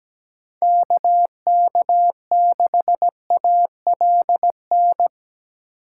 Morse code, heard as a keyed tone, about 17 wpm, 700 Hz. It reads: K K6ALN
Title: USA Call Signs - 8x Character Spacing - 17wpm